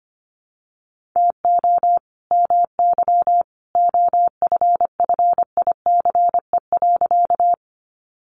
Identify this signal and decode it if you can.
Morse code, heard as a keyed tone, about 25 wpm, 700 Hz. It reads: TO MY OFFICE.
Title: Sentences from Top 1000 Words 25wpm